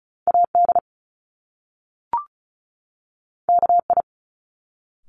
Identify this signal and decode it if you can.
Morse code, heard as a keyed tone, about 36 wpm, 700 Hz. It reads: AD KI
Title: Top Two Letter Combinations in Words 35wpm